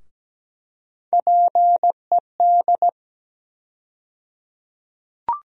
Morse code, heard as a keyed tone, about 17 wpm, 700 Hz. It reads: PED
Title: USA Call Sign Postfixes - 3 Letters 17wpm